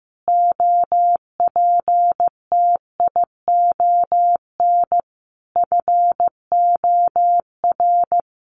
Morse code, heard as a keyed tone, about 15 wpm, 700 Hz. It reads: OPTION FOR
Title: Sentences from Top 1300 Words 15wpm